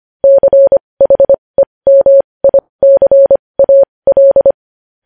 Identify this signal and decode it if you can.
Morse code, heard as a keyed tone, about 25 wpm, spaced as for 20 wpm, 550 Hz. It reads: CHEMICAL
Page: News Headlines in Morse Code at 20 WPM